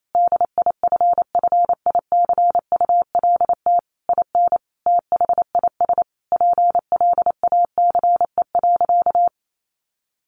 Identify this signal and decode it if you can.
Morse code, heard as a keyed tone, about 28 wpm, 700 Hz. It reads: DIFFICULT IN THIS PLACE.
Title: Sentences from Top 1500 Words 28wpm